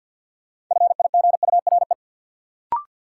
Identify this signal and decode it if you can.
Morse code, heard as a keyed tone, about 49 wpm, 700 Hz. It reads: FIGURE
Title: Sentences from Top 1400 Words - New Word Drill Alt - 50wpm